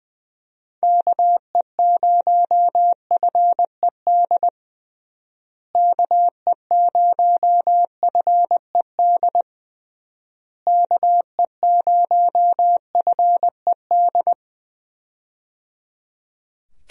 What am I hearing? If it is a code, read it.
Morse code, heard as a keyed tone, about 20 wpm, 700 Hz. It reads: KE0FED KE0FED KE0FED
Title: USA Call Signs - 6 Characters - Repeated 3x - 20wpm